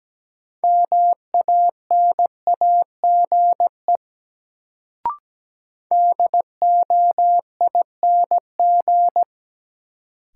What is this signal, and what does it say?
Morse code, heard as a keyed tone, about 17 wpm, 700 Hz. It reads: MANAGE DOING
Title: Sentences from Top 1400 Words - New Word Drill - 17wpm